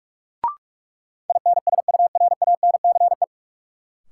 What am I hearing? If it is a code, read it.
Morse code, heard as a keyed tone, about 41 wpm, 700 Hz. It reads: INSURANCE
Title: Sentences from Top 1100 Words - New Word Drill - 45wpm